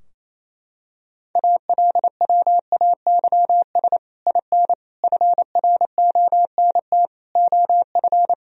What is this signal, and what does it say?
Morse code, heard as a keyed tone, about 28 wpm, 700 Hz. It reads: ALWAYS IN FRONT OF
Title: Sentences from Top 1000 Words - Review Part 2 - 28wpm